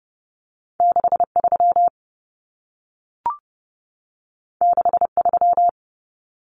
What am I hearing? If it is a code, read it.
Morse code, heard as a keyed tone, about 30 wpm, 700 Hz. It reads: 63 63
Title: Numbers - 2-Digit 30wpm